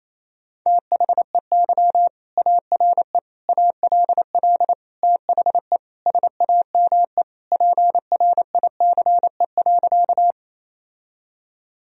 Morse code, heard as a keyed tone, about 28 wpm, 700 Hz. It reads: THEY ARE ALL THE SAME PRICE.
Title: Sentences from Top 1000 Words - Review Part 4 - 28wpm